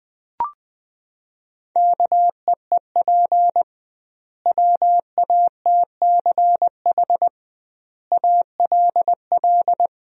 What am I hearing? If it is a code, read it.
Morse code, heard as a keyed tone, about 20 wpm, 700 Hz. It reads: KEEP WATCH ALL
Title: 2x Word-Spacing - Sentences from Top 300 Words - 20wpm